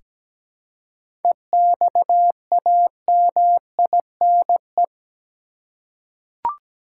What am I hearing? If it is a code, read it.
Morse code, heard as a keyed tone, about 17 wpm, 700 Hz. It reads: EXAMINE
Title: Seven Letter Words - 17wpm